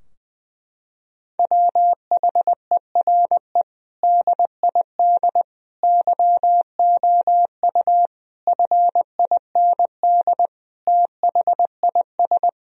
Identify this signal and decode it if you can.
Morse code, heard as a keyed tone, about 20 wpm, 700 Hz. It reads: WHERE DID YOU FIND THIS
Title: Sentences from Top 1000 Words - Review Part 2 - 20wpm